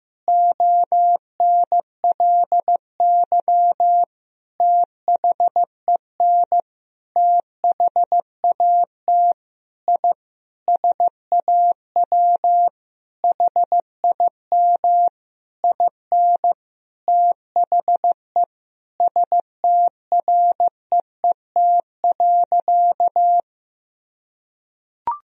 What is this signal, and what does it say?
Morse code, heard as a keyed tone, about 15 wpm, 700 Hz. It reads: ONLY THEN THAT I SAW HIM IN THE STREET.
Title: Sentences from Top 1000 Words - Review Part 3 - 15wpm